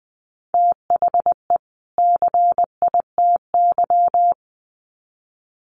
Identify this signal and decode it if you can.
Morse code, heard as a keyed tone, about 20 wpm, 700 Hz. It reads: THE CITY
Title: Sets of 2 Words 20wpm